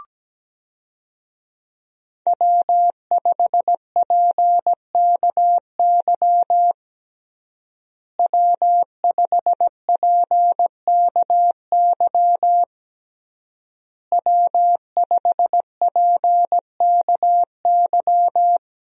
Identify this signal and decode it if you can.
Morse code, heard as a keyed tone, about 17 wpm, 700 Hz. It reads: W5PKY W5PKY W5PKY
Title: USA Call Signs - 5 Characters - Repeated 3x - 17wpm